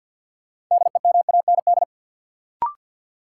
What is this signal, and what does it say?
Morse code, heard as a keyed tone, about 48 wpm, 700 Hz. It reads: DEMAND
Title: Sentences from Top 800 Words - New Word Drill - 50wpm